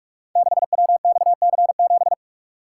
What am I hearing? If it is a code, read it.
Morse code, heard as a keyed tone, about 47 wpm, 690 Hz. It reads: 6WXC7
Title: Code Groups - 5 Letter:Numbers - 45wpm